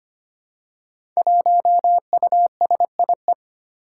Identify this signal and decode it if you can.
Morse code, heard as a keyed tone, about 25 wpm, 700 Hz. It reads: 1USIE
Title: Five Challenging Letter-Number Combinations 25wpm